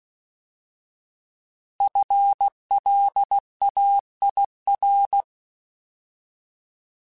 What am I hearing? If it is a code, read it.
Morse code, heard as a keyed tone, about 16 wpm, 800 Hz. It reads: FLAIR
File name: Alphabetical / Alphabetical mp3s / Fs @15